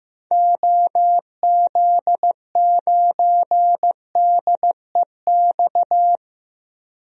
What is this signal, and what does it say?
Morse code, heard as a keyed tone, about 15 wpm, 700 Hz. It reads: OZ9DEX
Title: DXCC-100 Call Signs - 6 Characters - 15wpm